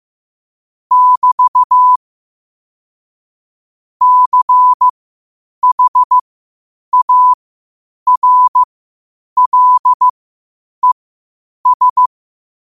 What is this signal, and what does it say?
Morse code, heard as a keyed tone, about 15 wpm, spaced as for 8 wpm, 1000 Hz. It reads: <BT> CHARLES
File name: qotdfiles / 8wpm